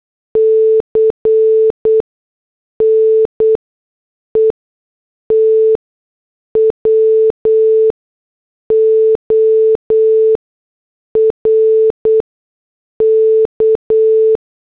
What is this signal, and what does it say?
Morse code, heard as a keyed tone, about 8 wpm, spaced as for 6 wpm, 440 Hz. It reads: C NETWORK
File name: morse / headlines